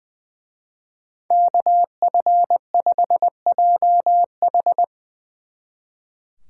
Morse code, heard as a keyed tone, about 22 wpm, 700 Hz. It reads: KF5JH
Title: Speed Racing - USA 2x2 Call Signs 20wpm